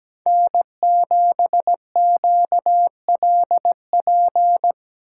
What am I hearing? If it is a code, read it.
Morse code, heard as a keyed tone, about 19 wpm, 700 Hz. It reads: N7QLP
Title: Speed Racing - USA 1x3 Call Signs 17wpm